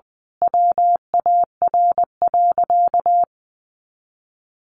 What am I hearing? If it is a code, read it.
Morse code, heard as a keyed tone, about 20 wpm, 700 Hz. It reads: WAR.
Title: Sentences from Top 1400 Words 20wpm